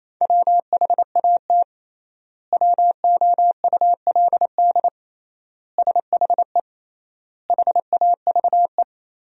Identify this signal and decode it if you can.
Morse code, heard as a keyed tone, about 28 wpm, 700 Hz. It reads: WHAT WOULD SHE HAVE
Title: 3x Word-Spacing - Sentences from Top 100 Words - 28wpm